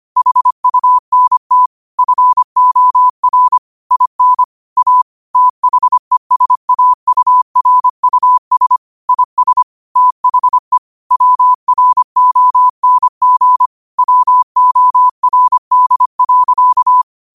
Morse code, heard as a keyed tone, about 25 wpm, 1000 Hz. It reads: HUNT FOR IN A THESAURUS IS THE WRONG WORD.
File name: qotdfiles / 25wpm